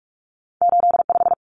Morse code, heard as a keyed tone, about 46 wpm, 700 Hz. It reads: 85
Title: Numbers - 2-Digit 45wpm